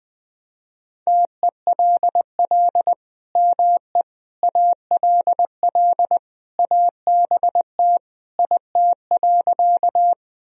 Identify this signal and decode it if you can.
Morse code, heard as a keyed tone, about 21 wpm, 700 Hz. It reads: TELL ME ALL ABT IT.
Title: Sentences from Top 200 Words and Abbreviations 20wpm